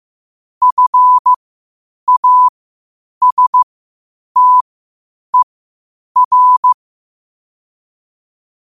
Morse code, heard as a keyed tone, about 15 wpm, spaced as for 8 wpm, 1000 Hz. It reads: FASTER
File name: qotdfiles / 8wpm